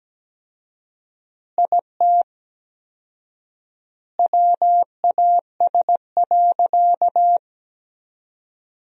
Morse code, heard as a keyed tone, about 17 wpm, 700 Hz. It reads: IT WAS.
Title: 4x Word-Spacing - Sentences from Top 100 Words - 17wpm